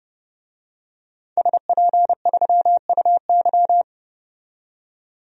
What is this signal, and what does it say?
Morse code, heard as a keyed tone, about 30 wpm, 700 Hz. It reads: SP3UY
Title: DXCC-100 Call Signs - 5 Characters - 30wpm